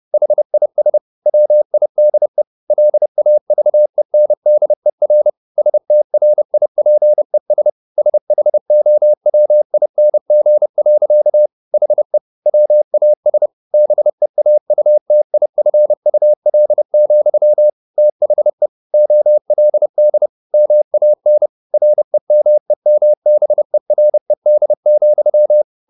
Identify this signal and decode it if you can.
Morse code, heard as a keyed tone, about 30 wpm, 600 Hz. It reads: HIS WIDE LAVENDER STRIPES SHOWING. HE WAS BEAUTIFUL, THE OLD MAN REMEMBERED,
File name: the old man and the sea / the old man and the sea 1 30wpm